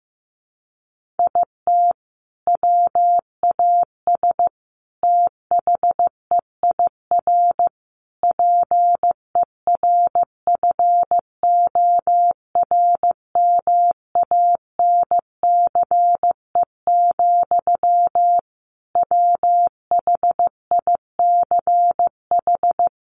Morse code, heard as a keyed tone, about 15 wpm, 700 Hz. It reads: IT WAS THEIR PERFORMANCE, WHICH
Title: Sentences from Top 1500 Words 15wpm